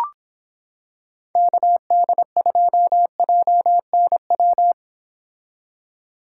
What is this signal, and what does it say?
Morse code, heard as a keyed tone, about 26 wpm, 700 Hz. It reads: KD2JNW